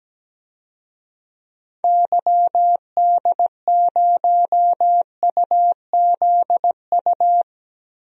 Morse code, heard as a keyed tone, about 17 wpm, 700 Hz. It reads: YD0UZU
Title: DXCC-100 Call Signs - 6 Characters - Repeated 3x - 17wpm